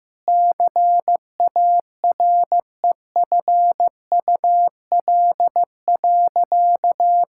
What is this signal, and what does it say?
Morse code, heard as a keyed tone, about 15 wpm, 700 Hz. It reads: CAREFUL.
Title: Sentences from Top 1600 Words 15wpm